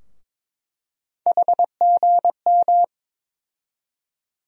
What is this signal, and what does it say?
Morse code, heard as a keyed tone, about 21 wpm, 700 Hz. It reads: HGM
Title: USA Call Sign Postfixes - 3 Letters 22wpm